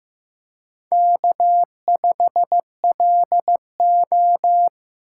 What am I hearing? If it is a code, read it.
Morse code, heard as a keyed tone, about 15 wpm, 700 Hz. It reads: K5LO
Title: USA Call Signs - 4 Characters - 4x Character Spacing - 15wpm